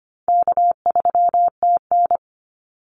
Morse code, heard as a keyed tone, about 25 wpm, 700 Hz. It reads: K3TN
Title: Call Signs - 25wpm